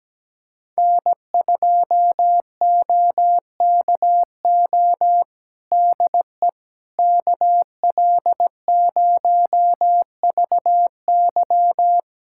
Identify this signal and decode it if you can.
Morse code, heard as a keyed tone, about 17 wpm, 700 Hz. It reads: N2OKO DE KL0VY
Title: USA Callsign DE Callsign 17wpm